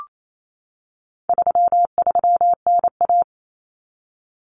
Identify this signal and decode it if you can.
Morse code, heard as a keyed tone, about 28 wpm, 700 Hz. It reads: 33NA
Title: Common Cut Numbers - 4-Digit 28wpm